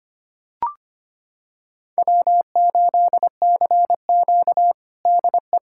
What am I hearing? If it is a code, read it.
Morse code, heard as a keyed tone, about 25 wpm, 700 Hz. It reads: W8CQ DE
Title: USA Callsign DE Callsign - 4 Characters - 25wpm